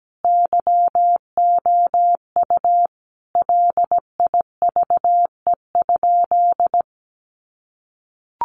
Morse code, heard as a keyed tone, about 17 wpm, 700 Hz. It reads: YOU LIVE?
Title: Sentences from Top 200 Words 17wpm